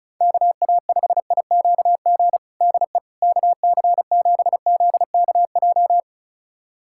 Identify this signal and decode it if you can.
Morse code, heard as a keyed tone, about 36 wpm, 700 Hz. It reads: KA5IQG DE KC7ZKJ
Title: USA Callsign DE Callsign - 6 Characters - 35wpm